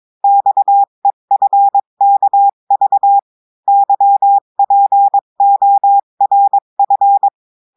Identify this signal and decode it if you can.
Morse code, heard as a keyed tone, about 22 wpm, 800 Hz. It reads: XEFKV YPORF
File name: cours-du-mercredi / audios / cours-853-du-2-mai-2024-etxte chiffre et texte clair-a-20-wpm